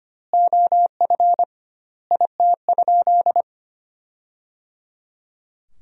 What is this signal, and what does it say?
Morse code, heard as a keyed tone, about 25 wpm, 700 Hz. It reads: OF IT?
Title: 2x Word-Spacing - Sentences from Top 300 Words - 25wpm